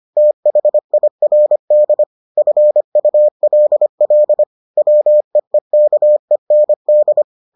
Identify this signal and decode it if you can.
Morse code, heard as a keyed tone, about 25 wpm, 600 Hz. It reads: THIRD FULL WEEKEND